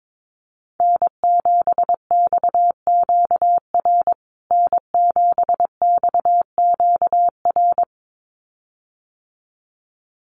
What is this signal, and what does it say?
Morse code, heard as a keyed tone, about 22 wpm, 700 Hz. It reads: N7XQR N7XQR
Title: USA Call Signs Repeated 2x 22wpm